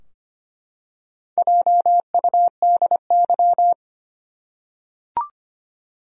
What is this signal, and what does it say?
Morse code, heard as a keyed tone, about 25 wpm, 700 Hz. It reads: JUDY
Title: USA Common Names - 25wpm